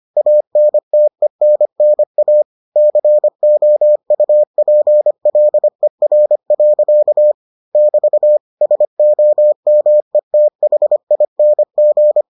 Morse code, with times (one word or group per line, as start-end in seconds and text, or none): 0.2-2.4 ANTENNA
2.8-7.3 COUPLER.
7.8-8.4 <BT>
8.6-12.2 SOMETHING